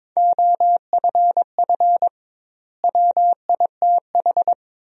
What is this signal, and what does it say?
Morse code, heard as a keyed tone, about 22 wpm, 700 Hz. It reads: OFF WITH
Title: 2x Word-Spacing - Sentences from Top 300 Words - 22wpm